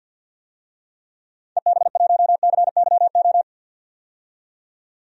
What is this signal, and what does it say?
Morse code, heard as a keyed tone, about 50 wpm, 700 Hz. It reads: EB1CYO
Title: DXCC-100 Call Signs - 6 Characters - 50wpm